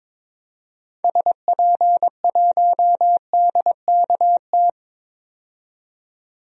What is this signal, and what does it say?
Morse code, heard as a keyed tone, about 21 wpm, 700 Hz. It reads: SP1DKT
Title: DXCC-100 Call Signs - 6 Characters - 22wpm